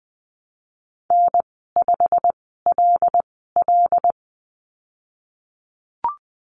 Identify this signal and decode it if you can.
Morse code, heard as a keyed tone, about 19 wpm, spaced as for 14 wpm, 700 Hz. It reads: N5LL
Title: 128 Lesson 21g - Number 6 - Call Signs 20wpm